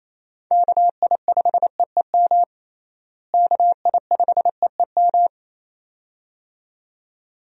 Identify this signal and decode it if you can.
Morse code, heard as a keyed tone, about 27 wpm, 700 Hz. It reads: KI5EEM KI5EEM
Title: USA Call Signs - 6 Characters - Repeated 2x - 28wpm